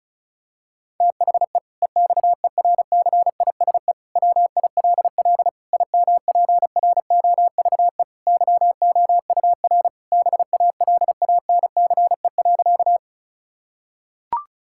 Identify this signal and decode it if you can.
Morse code, heard as a keyed tone, about 36 wpm, 700 Hz. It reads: THE EXERCISE WILL IMPROVE YOUR BALANCE.
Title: Sentences from Top 1000 Words - Review Part 6 - 35wpm